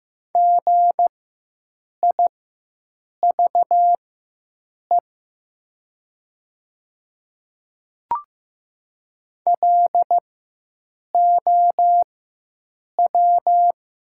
Morse code, heard as a keyed tone, about 15 wpm, spaced as for 7 wpm, 700 Hz. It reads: GIVE LOW
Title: 4x Character Spacing - Top 200 Words - 15wpm